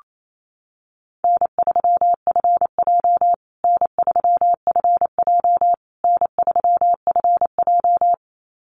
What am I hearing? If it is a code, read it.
Morse code, heard as a keyed tone, about 28 wpm, 700 Hz. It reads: N3FJ N3FJ N3FJ
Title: USA Call Signs Repeated 3x 28wpm